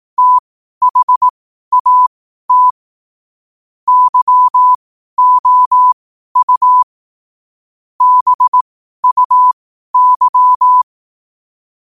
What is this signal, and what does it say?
Morse code, heard as a keyed tone, about 18 wpm, spaced as for 12 wpm, 1000 Hz. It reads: THAT YOU BUY
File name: qotdfiles / 12wpm